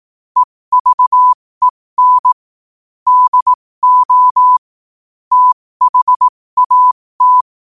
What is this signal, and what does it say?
Morse code, heard as a keyed tone, about 18 wpm, spaced as for 15 wpm, 1000 Hz. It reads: EVEN DO THAT
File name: qotdfiles / 15wpm